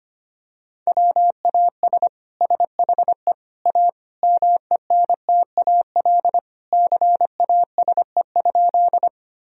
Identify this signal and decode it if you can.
Morse code, heard as a keyed tone, about 25 wpm, 700 Hz. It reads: WAS SHE A MENTAL CASE?
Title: Sentences from Top 1000 Words - Review Part 4 - 25wpm